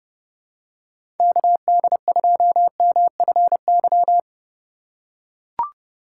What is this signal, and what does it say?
Morse code, heard as a keyed tone, about 31 wpm, 700 Hz. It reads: KD2MFY
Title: USA Call Signs - 6 Characters - 4x Character Spacing - 30wpm